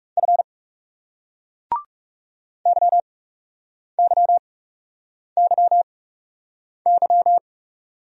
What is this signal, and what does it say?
Morse code, heard as a keyed tone, about 38 wpm, spaced as for 8 wpm, 700 Hz. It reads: F YYYY